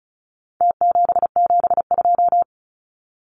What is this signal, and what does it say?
Morse code, heard as a keyed tone, about 34 wpm, 700 Hz. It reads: T772
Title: Numbers - Contest Style - 4-Digit 35wpm